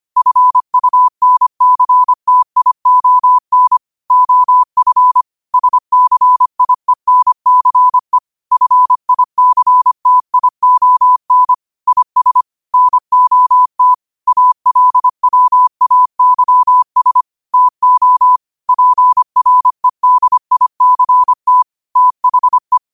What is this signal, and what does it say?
Morse code, heard as a keyed tone, about 25 wpm, 1000 Hz. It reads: FUNCTION OF SCIENCE FICTION IS NOT ALWAYS TO PREDICT THE
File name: qotdfiles / 25wpm